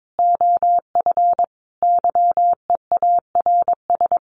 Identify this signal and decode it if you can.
Morse code, heard as a keyed tone, about 22 wpm, 700 Hz. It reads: OF YEARS
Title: Sets of 5 Words 22wpm